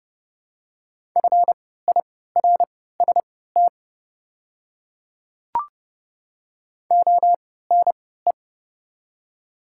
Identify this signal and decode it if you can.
Morse code, heard as a keyed tone, about 30 wpm, spaced as for 17 wpm, 700 Hz. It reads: FIRST ONE
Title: 3x Character Spacing - Top 100 Words - 30wpm